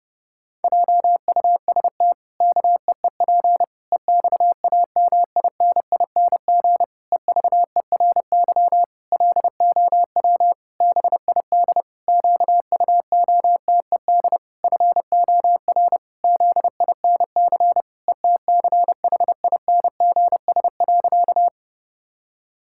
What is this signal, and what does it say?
Morse code, heard as a keyed tone, about 30 wpm, 700 Hz. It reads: JUST KEEP EXAMINING EVERY LOW BID QUOTED FOR ZINC ETCHINGS.
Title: Pangram Challenge v2 - 30wpm